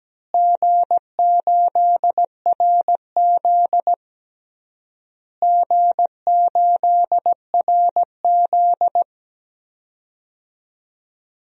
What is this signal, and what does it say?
Morse code, heard as a keyed tone, about 17 wpm, 700 Hz. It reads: G8RZ G8RZ
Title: DXCC-100 Call Signs - 4 Characters - Repeated 3x - 17wpm